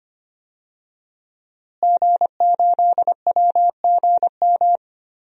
Morse code, heard as a keyed tone, about 25 wpm, 700 Hz. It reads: G8WGM
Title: DXCC-100 Call Signs - 5 Characters - Repeated 3x - 25wpm